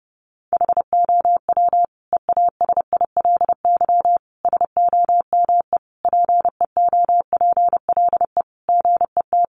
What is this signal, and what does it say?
Morse code, heard as a keyed tone, about 30 wpm, 700 Hz. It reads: HOW EASILY SOME PEOPLE GET
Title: Sentences from Top 900 Words 30wpm